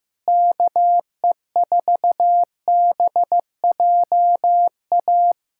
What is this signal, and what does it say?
Morse code, heard as a keyed tone, about 15 wpm, 700 Hz. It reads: KE4BJA
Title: USA Callsign DE Callsign - 6 Characters - 15wpm